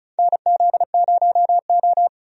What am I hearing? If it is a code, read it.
Morse code, heard as a keyed tone, about 36 wpm, 700 Hz. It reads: NZ0O
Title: USA Call Signs - 4 Characters - Repeated 2x - 35wpm